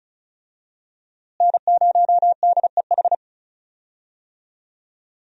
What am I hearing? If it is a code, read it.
Morse code, heard as a keyed tone, about 36 wpm, 700 Hz. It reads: N0DEH
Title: USA Call Signs - 5 Characters - Repeated 3x - 35wpm